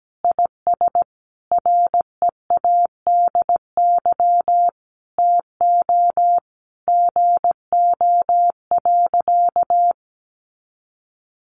Morse code, heard as a keyed tone, about 17 wpm, 700 Hz. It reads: IS READY TO GO.
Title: Sentences from Top 700 Words 17wpm